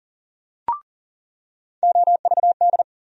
Morse code, heard as a keyed tone, about 40 wpm, 700 Hz. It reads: OVD
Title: USA Call Sign Postfixes - 3 Letters 40wpm